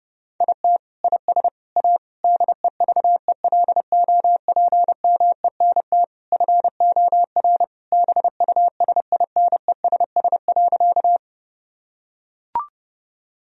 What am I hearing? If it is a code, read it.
Morse code, heard as a keyed tone, about 30 wpm, 700 Hz. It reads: IT IS A DEVELOPMENT FOR BUSINESS.
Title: Sentences from Top 1000 Words - Review Part 4 - 30wpm